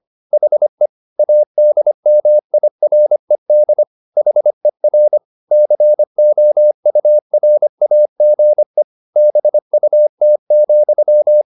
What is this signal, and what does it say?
Morse code, heard as a keyed tone, about 25 wpm, 600 Hz. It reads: HE ADMIRED HER COURAGE BUT,